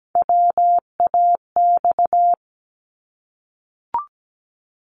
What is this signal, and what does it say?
Morse code, heard as a keyed tone, about 17 wpm, 700 Hz. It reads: WAX
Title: Multi-letter Phonemes 17wpm